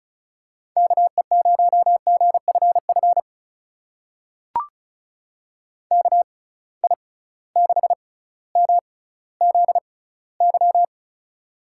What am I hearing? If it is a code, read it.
Morse code, heard as a keyed tone, about 36 wpm, spaced as for 18 wpm, 700 Hz. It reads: KE0GFF KI6MZY
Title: USA Call Signs - 6 Characters - 6x Character Spacing - 35wpm